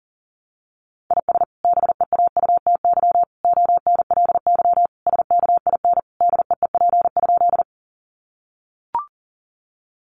Morse code, heard as a keyed tone, about 40 wpm, 700 Hz. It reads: IS BEAUTY ONLY SKIN DEEP?
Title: Sentences from Top 1300 Words 40wpm